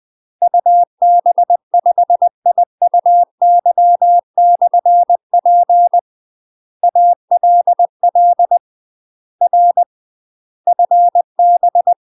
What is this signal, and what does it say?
Morse code, heard as a keyed tone, about 20 wpm, 700 Hz. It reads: UB5IUY/P ALL R FB
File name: UK MASTER QSOs TRAINING / 20WPM / UK MASTER QSOs 0047